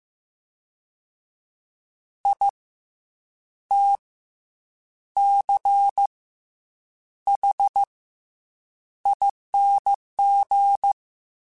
Morse code, heard as a keyed tone, about 15 wpm, 800 Hz. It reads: I T C H ING